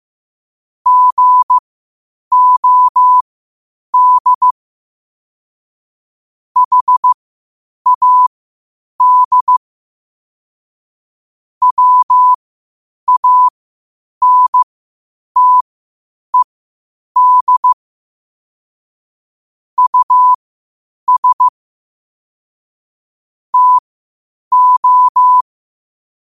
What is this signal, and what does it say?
Morse code, heard as a keyed tone, about 15 wpm, spaced as for 8 wpm, 1000 Hz. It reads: GOD HAD WANTED US TO
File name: qotdfiles / 8wpm